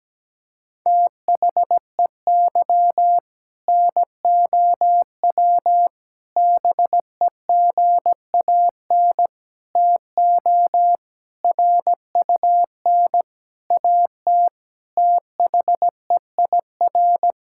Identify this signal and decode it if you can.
Morse code, heard as a keyed tone, about 17 wpm, 700 Hz. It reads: THEY NOW BEGAN TO RUN AT THEIR
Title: Sentences from Top 1000 Words - Review Part 5 - 17wpm